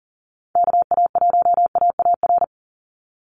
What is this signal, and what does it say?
Morse code, heard as a keyed tone, about 40 wpm, 700 Hz. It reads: KA1AAR
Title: USA Call Signs 40wpm